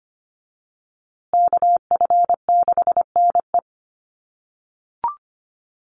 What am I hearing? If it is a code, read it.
Morse code, heard as a keyed tone, about 25 wpm, 700 Hz. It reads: KF6NE
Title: USA Call Signs Repeated 3x 25wpm